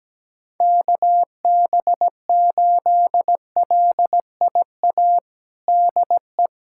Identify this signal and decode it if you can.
Morse code, heard as a keyed tone, about 17 wpm, 700 Hz. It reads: KB8LIA DE
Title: USA Callsign DE Callsign - 6 Characters - 17wpm